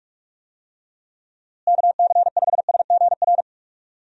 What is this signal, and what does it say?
Morse code, heard as a keyed tone, about 45 wpm, 700 Hz. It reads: KC5SGR
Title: USA Call Signs - 6 Characters - Repeated 3x - 45wpm